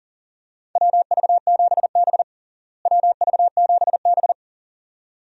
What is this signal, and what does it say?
Morse code, heard as a keyed tone, about 40 wpm, 700 Hz. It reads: WV7B WV7B
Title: USA Call Signs - 4 Characters - Repeated 2x - 40wpm